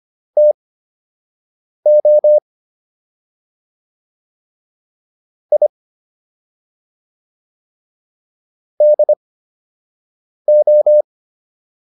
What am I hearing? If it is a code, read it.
Morse code, heard as a keyed tone, about 25 wpm, spaced as for 6 wpm, 600 Hz. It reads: TO I DO